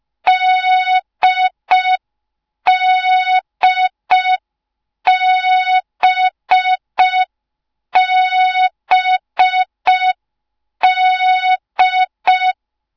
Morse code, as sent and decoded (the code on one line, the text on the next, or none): -.. -.. -... -... -..
DDBBD